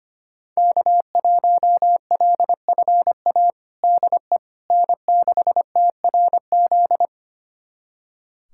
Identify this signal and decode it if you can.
Morse code, heard as a keyed tone, about 25 wpm, 700 Hz. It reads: K1LFA DE N6TRZ
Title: USA Callsign DE Callsign - 5 Characters - 25wpm